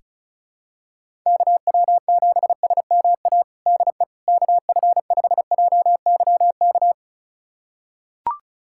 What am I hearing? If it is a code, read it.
Morse code, heard as a keyed tone, about 36 wpm, 700 Hz. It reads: KW7SMA DE KF5JYK
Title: USA Callsign DE Callsign 35wpm